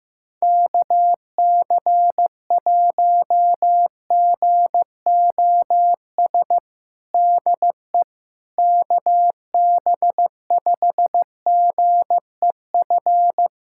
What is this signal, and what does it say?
Morse code, heard as a keyed tone, about 15 wpm, 700 Hz. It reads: KC1GOS DE KB5GEF
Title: USA Callsign DE Callsign - 6 Characters - 15wpm